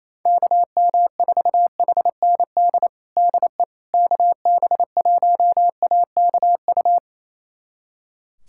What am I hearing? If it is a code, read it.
Morse code, heard as a keyed tone, about 28 wpm, 700 Hz. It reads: KM4HND DE KB1AKU
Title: USA Callsign DE Callsign 28wpm